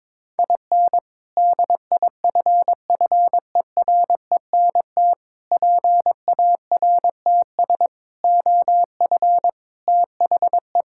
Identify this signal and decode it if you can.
Morse code, heard as a keyed tone, about 22 wpm, 700 Hz. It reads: IN DIFFERENT PARTS OF THE
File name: practice / 22wpm / Sentences with 3-Word Phrases - 22wpm